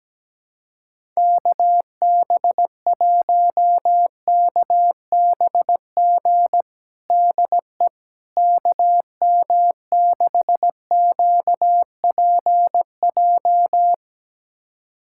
Morse code, heard as a keyed tone, about 17 wpm, 700 Hz. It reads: KB1KBG DE KM6QPJ
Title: USA Callsign DE Callsign 17wpm